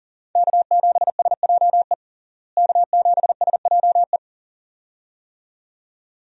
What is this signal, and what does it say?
Morse code, heard as a keyed tone, about 40 wpm, 700 Hz. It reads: K7SJE K7SJE
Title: USA Call Signs - 5 Characters - Repeated 2x - 40wpm